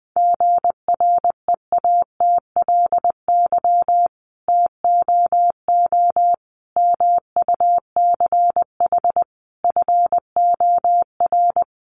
Morse code, heard as a keyed tone, about 20 wpm, 700 Hz. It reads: GREATLY TOO MUCH FOR
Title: Sentences from Top 1400 Words 20wpm